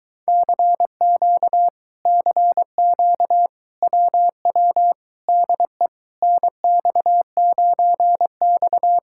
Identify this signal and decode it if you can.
Morse code, heard as a keyed tone, about 23 wpm, 700 Hz. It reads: CQ CQ WW DE NX9X